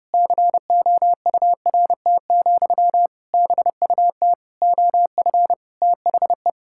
Morse code, heard as a keyed tone, about 30 wpm, 700 Hz. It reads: COURT, BUT OF THE
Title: Sentences from Top 1600 Words 30wpm